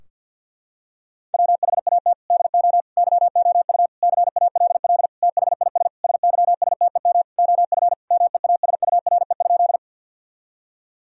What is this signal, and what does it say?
Morse code, heard as a keyed tone, about 50 wpm, 700 Hz. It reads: WHAT DO YOU CALL THIS SYSTEM OF MEASURE?
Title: Sentences from Top 1000 Words - Review Part 3 - 50wpm